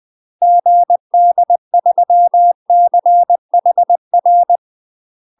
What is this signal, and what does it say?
Morse code, heard as a keyed tone, about 20 wpm, 700 Hz. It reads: GD3CHR
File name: UK MASTER QSOs TRAINING / 20WPM / UK MASTER QSOs 0034